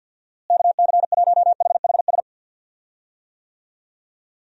Morse code, heard as a keyed tone, about 51 wpm, 700 Hz. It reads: KC1HHS
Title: USA Call Signs - 6 Characters - Repeated 3x - 50wpm